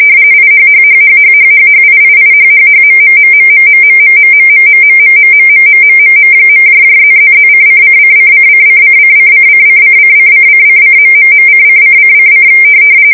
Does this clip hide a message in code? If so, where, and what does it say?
Yes: RTTY from 0.3 to 12.5 s, about 45.45 baud. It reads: IHHHHHHHHIIIHHHMMMMMMMMMMMMMMMMMMMMM.+++:::::::::::::::::::::...::::::